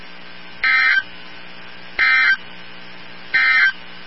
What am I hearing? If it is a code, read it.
Emergency Alert System (SAME) tones, heard as harsh buzzing bursts. It is heard from 0.5 to 3.8 s.
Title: NOAA Weather Radio 162.550